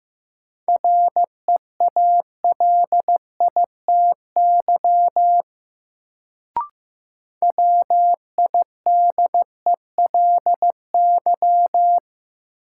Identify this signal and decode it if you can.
Morse code, heard as a keyed tone, about 15 wpm, 700 Hz. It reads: REALITY WIDELY
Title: Sentences from Top 1200 Words - New Word Drill - 15wpm